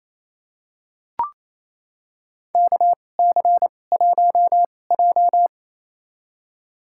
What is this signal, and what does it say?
Morse code, heard as a keyed tone, about 28 wpm, spaced as for 20 wpm, 700 Hz. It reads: KC1J
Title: USA Call Signs - 4 Characters - 2x Character Spacing - 28wpm